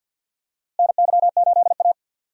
Morse code, heard as a keyed tone, about 52 wpm, 700 Hz. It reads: NY8A
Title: USA Call Signs - 4 Characters - Repeated 3x - 50wpm